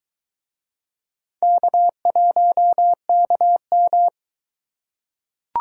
Morse code, heard as a keyed tone, about 23 wpm, 700 Hz. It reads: K1KM